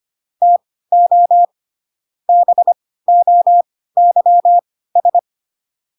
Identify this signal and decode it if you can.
Morse code, heard as a keyed tone, about 25 wpm, spaced as for 16 wpm, 700 Hz. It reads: TO BOYS
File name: Pinocchio / Pinocchio ch4 10 25 ik1hge